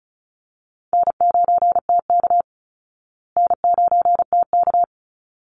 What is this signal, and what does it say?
Morse code, heard as a keyed tone, about 34 wpm, 700 Hz. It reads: N9TK N9TK
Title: Easy Call Signs Repeated 2x - 35wpm